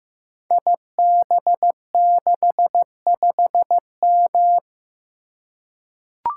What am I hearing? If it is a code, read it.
Morse code, heard as a keyed tone, about 15 wpm, 700 Hz. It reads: IB65M
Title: Code Groups - 5 Letter:Numbers - 15wpm